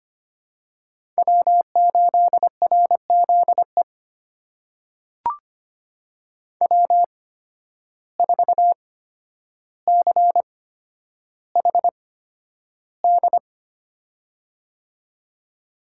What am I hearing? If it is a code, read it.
Morse code, heard as a keyed tone, about 25 wpm, spaced as for 11 wpm, 700 Hz. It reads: W8RZE W4CHD
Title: USA Call Signs - 5 Characters - 8x Character Spacing - 25wpm